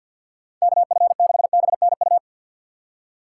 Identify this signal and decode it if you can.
Morse code, heard as a keyed tone, about 46 wpm, 700 Hz. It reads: KF6BNU
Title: USA Call Signs - 2x Character Spacing - 50wpm